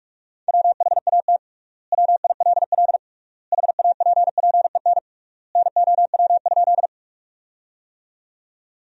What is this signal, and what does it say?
Morse code, heard as a keyed tone, about 46 wpm, 700 Hz. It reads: WHAT WILL HAPPEN NOW?
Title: 3x Word-Spacing - Sentences from Top 400 Words - 45wpm